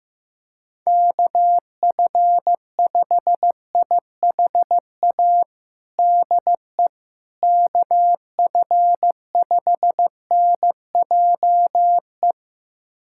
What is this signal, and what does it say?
Morse code, heard as a keyed tone, about 15 wpm, 700 Hz. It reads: KF5IHA DE KF5NJE
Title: USA Callsign DE Callsign 15wpm